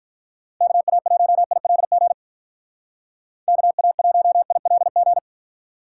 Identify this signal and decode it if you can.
Morse code, heard as a keyed tone, about 41 wpm, 700 Hz. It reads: KA1ILG KA1ILG
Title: Speed Racing - USA 2x3 Call Signs 35wpm